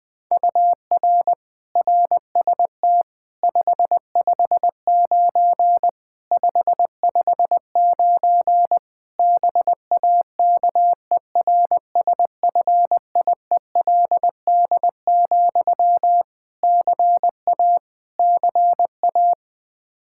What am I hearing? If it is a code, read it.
Morse code, heard as a keyed tone, about 20 wpm, 700 Hz. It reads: UR RST 559 559 BAKERSFIELD, CA CA
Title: K1USN SST QSO Practice 20wpm